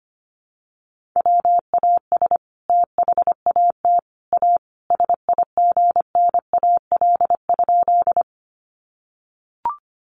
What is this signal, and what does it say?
Morse code, heard as a keyed tone, about 25 wpm, 700 Hz. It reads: WAS THAT A SIGNAL?
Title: Sentences from Top 1400 Words 25wpm